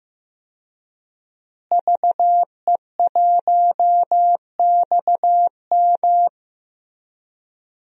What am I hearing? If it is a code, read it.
Morse code, heard as a keyed tone, about 15 wpm, 700 Hz. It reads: VE1XM
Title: DXCC-100 Call Signs - 5 Characters - Repeated 2x - 15wpm